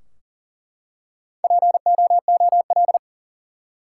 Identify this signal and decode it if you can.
Morse code, heard as a keyed tone, about 40 wpm, 700 Hz. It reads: POOL
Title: Four Letter Words - 40wpm